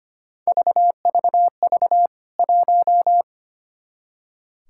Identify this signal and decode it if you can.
Morse code, heard as a keyed tone, about 25 wpm, 700 Hz. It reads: VVV 1
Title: K1USN SST QSO Practice 25wpm